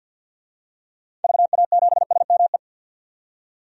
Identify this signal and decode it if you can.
Morse code, heard as a keyed tone, about 45 wpm, 700 Hz. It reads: VA7SME
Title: DXCC-100 Call Signs - 6 Characters - 50wpm